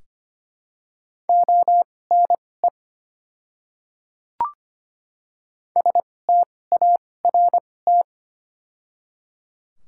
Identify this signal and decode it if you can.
Morse code, heard as a keyed tone, about 25 wpm, spaced as for 18 wpm, 700 Hz. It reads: ONE START
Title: 2x Character Spacing - Top 300 Words - 25wpm